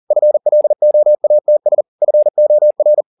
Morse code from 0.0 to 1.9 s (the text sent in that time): FLOATS